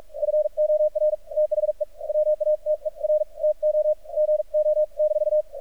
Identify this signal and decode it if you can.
Morse code, heard as a keyed tone, about 46 wpm, spaced as for 40 wpm, 610 Hz. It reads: FOR THE WATER TO GO-
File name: archived / CRSnet